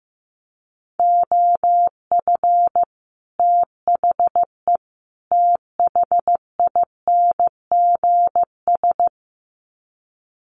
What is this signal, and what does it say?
Morse code, heard as a keyed tone, about 14 wpm, 700 Hz. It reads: OF THE THINGS
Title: Sets of 3 Words 15wpm